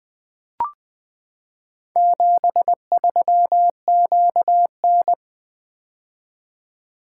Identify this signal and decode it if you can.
Morse code, heard as a keyed tone, about 20 wpm, 700 Hz. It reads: 73QN
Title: Four Letter-Number Combinations - 20wpm